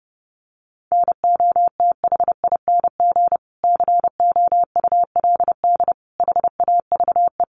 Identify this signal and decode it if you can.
Morse code, heard as a keyed tone, about 30 wpm, 700 Hz. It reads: NOTHING COULD HAVE
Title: Sentences from Top 1300 Words 30wpm